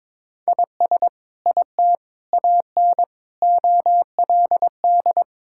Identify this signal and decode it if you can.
Morse code, heard as a keyed tone, about 22 wpm, 700 Hz. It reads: IS IT AN OLD
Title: Sentences from Top 1000 Words - Review Part 5 - 22wpm